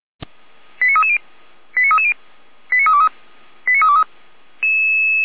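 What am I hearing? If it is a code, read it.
Five-tone selective call (ZVEI), heard as quick runs of five tones, tones of about 70 ms. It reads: 89330 89330 89323 89323
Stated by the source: ZVEI five-tone selective call 88080, received via SDR on channel 455.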